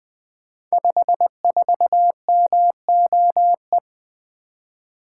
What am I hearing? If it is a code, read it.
Morse code, heard as a keyed tone, about 20 wpm, 700 Hz. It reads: 54MOE
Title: Code Groups - 5 Letter:Numbers - 20wpm